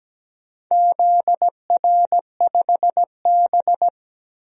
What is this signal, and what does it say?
Morse code, heard as a keyed tone, about 17 wpm, 700 Hz. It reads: ZR5B
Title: DXCC-100 Call Signs - 4 Characters - 17wpm